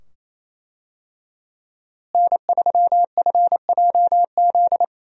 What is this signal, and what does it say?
Morse code, heard as a keyed tone, about 28 wpm, 700 Hz. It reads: N3FJZ
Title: USA Call Signs - 5 Characters - Repeated 2x - 28wpm